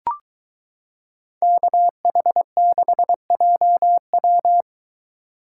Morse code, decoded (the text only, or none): KH6JW